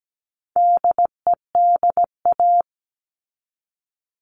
Morse code, heard as a keyed tone, about 17 wpm, 700 Hz. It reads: DEDA
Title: All Cut Numbers - 4-Digit 17wpm